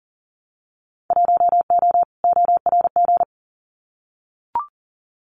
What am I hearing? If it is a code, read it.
Morse code, heard as a keyed tone, about 40 wpm, 700 Hz. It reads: 1O ORG
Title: Winter Field Day Exchanges 40wpm